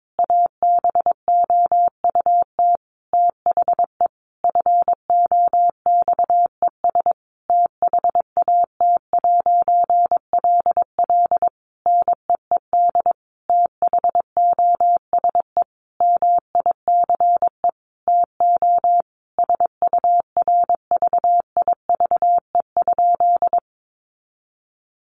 Morse code, heard as a keyed tone, about 22 wpm, 700 Hz. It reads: ABOUT THE FOXES THAT'LL NEED THOSE MICE TO SURVIVE?
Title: A Sound Of Thunder 22wpm